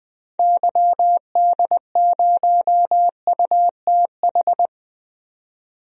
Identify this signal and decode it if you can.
Morse code, heard as a keyed tone, about 20 wpm, 700 Hz. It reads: YD0UTH
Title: DXCC-100 Call Signs - 6 Characters - 20wpm